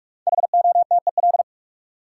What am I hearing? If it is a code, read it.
Morse code, heard as a keyed tone, about 46 wpm, 700 Hz. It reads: HOTEL
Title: Sentences from Top 1600 Words - New Word Drill - 45wpm